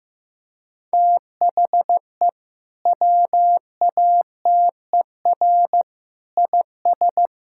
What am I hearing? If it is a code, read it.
Morse code, heard as a keyed tone, about 15 wpm, 700 Hz. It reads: THE WATER IS
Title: Sentences from Top 1000 Words - Review Part 5 - 15wpm